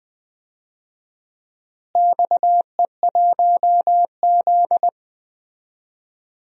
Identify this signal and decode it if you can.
Morse code, heard as a keyed tone, about 20 wpm, 700 Hz. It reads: XE1Z
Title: DXCC-100 Call Signs - 4 Characters - Repeated 2x - 20wpm